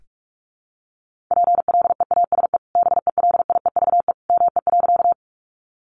Morse code, heard as a keyed tone, about 44 wpm, 700 Hz. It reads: PLEASE BELIEVE ME.
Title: Sentences from Top 1500 Words 45wpm